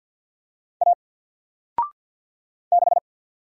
Morse code, heard as a keyed tone, about 48 wpm, 700 Hz. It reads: A 6
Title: Single Character (Letter:Number:Punctuation:Prosign) Alt 50wpm